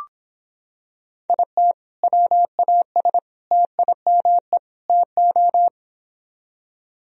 Morse code, heard as a keyed tone, about 25 wpm, 700 Hz. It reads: IT WAS TIME TO